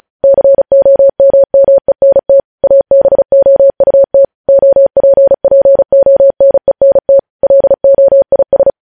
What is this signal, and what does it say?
Morse code, heard as a keyed tone, about 36 wpm, 550 Hz. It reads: COMMENT ABOUT OPPONENT LOIS
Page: news headlines in morse code at 35 wpm